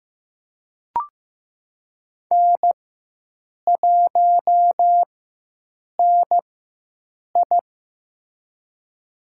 Morse code, heard as a keyed tone, about 15 wpm, spaced as for 7 wpm, 700 Hz. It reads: N1NI
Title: USA Call Signs - 4x Character Spacing - 15wpm